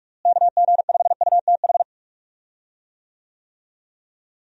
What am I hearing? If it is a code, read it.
Morse code, heard as a keyed tone, about 42 wpm, 700 Hz. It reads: KG5UTH
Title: USA Call Signs - 6 Characters - Repeated 3x - 45wpm